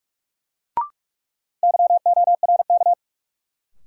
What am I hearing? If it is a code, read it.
Morse code, heard as a keyed tone, about 46 wpm, 700 Hz. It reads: YORK
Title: Sets of 2 Words - New Word Drill - 45wpm